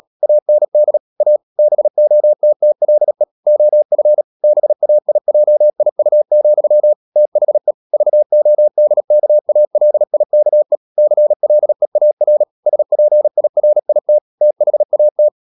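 Morse code, heard as a keyed tone, about 37 wpm, 600 Hz. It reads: AND A BOTTLE OF BAIJIU, THE VODKALIKE CLEAR SPIRIT THAT
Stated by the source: Morse code practice at 35–39 wpm sending English text of Ken Follett's novel Never, from CW med Gnister.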